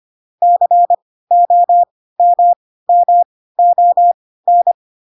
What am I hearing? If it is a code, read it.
Morse code, heard as a keyed tone, about 25 wpm, spaced as for 16 wpm, 700 Hz. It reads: COMMON
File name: Pinocchio / Pinocchio ch1 10 25 ik1hge